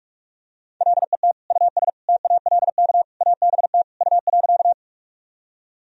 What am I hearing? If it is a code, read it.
Morse code, heard as a keyed tone, about 46 wpm, 700 Hz. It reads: LET US TALK ABT U.
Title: Sentences from Top 400 Words and Abbreviations 45wpm